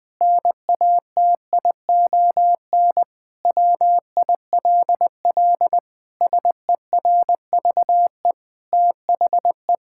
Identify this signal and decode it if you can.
Morse code, heard as a keyed tone, about 20 wpm, 700 Hz. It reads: NATION WILL SERVE THE
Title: Sentences from Top 1000 Words - Review Part 6 - 20wpm